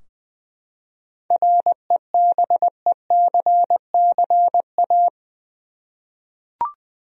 Morse code, heard as a keyed tone, about 20 wpm, 700 Hz. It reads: REBECCA
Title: USA Common Names - 20wpm